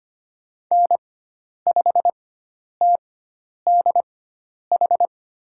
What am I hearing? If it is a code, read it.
Morse code, heard as a keyed tone, about 25 wpm, spaced as for 10 wpm, 700 Hz. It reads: N5TDH
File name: Lesson 16g - Letter c - Call Signs 25wpm